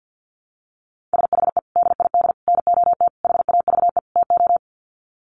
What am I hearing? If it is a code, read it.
Morse code, heard as a keyed tone, about 50 wpm, 700 Hz. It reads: SHE DID NOT HAVE TO
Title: Sets of 5 Words Encore2 50wpm